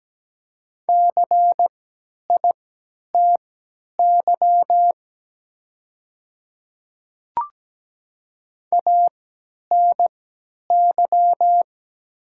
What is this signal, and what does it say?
Morse code, heard as a keyed tone, about 17 wpm, spaced as for 10 wpm, 700 Hz. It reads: CITY ANY
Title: 3x Character Spacing - Top 300 Words - 17wpm